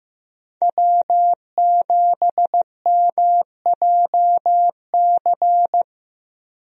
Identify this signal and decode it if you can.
Morse code, heard as a keyed tone, about 15 wpm, 700 Hz. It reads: W7MJC
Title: USA Call Signs - 5 Characters - 15wpm